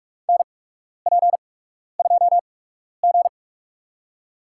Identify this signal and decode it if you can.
Morse code, heard as a keyed tone, about 42 wpm, spaced as for 12 wpm, 690 Hz. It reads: NP2G
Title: USA Call Signs - 4 Characters - 8x Character Spacing - 45wpm